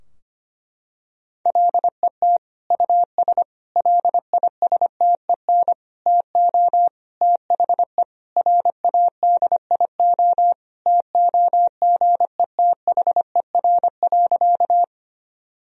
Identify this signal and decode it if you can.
Morse code, heard as a keyed tone, about 25 wpm, 700 Hz. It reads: LET US LISTEN TO THE RADIO TOGETHER.
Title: Sentences from Top 1000 Words - Review Part 3 - 25wpm